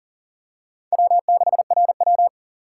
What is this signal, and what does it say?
Morse code, heard as a keyed tone, about 40 wpm, 700 Hz. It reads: W6RW
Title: USA Call Signs - 4 Characters - 6x Character Spacing - 40wpm